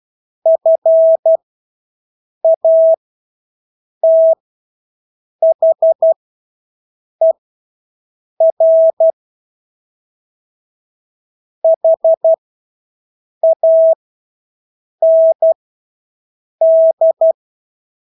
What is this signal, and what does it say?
Morse code, heard as a keyed tone, about 12 wpm, spaced as for 6 wpm, 650 Hz. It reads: FATHER HAND
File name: Engelske ord 12-6 wpm